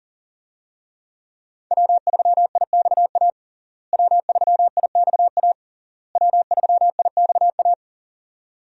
Morse code, heard as a keyed tone, about 40 wpm, 700 Hz. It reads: W3IXA W3IXA W3IXA
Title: USA Call Signs - 5 Characters - Repeated 3x - 40wpm